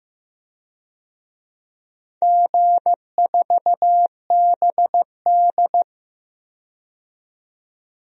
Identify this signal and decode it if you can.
Morse code, heard as a keyed tone, about 15 wpm, 700 Hz. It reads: G4BD
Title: DXCC-100 Call Signs - 4 Characters - 15wpm